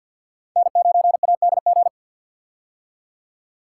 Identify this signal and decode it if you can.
Morse code, heard as a keyed tone, about 52 wpm, 700 Hz. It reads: N9ADG
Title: K1USN SST QSO Practice - 50wpm